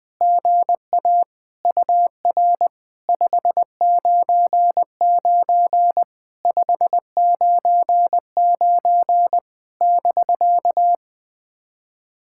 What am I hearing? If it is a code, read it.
Morse code, heard as a keyed tone, about 20 wpm, 700 Hz. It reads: GA UR 599 599 <BK>
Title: POTA QSO Practice - 20wpm